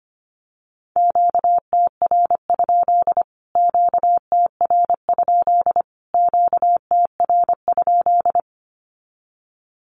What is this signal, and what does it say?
Morse code, heard as a keyed tone, about 25 wpm, 700 Hz. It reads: QTR? QTR? QTR?
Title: Q-Codes Repeated 3x 25wpm